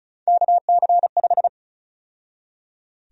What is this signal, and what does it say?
Morse code, heard as a keyed tone, about 34 wpm, 700 Hz. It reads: KC5
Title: USA Call Sign Prefixes - 35wpm